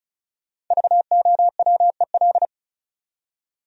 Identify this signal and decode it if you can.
Morse code, heard as a keyed tone, about 45 wpm, 700 Hz. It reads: VOWEL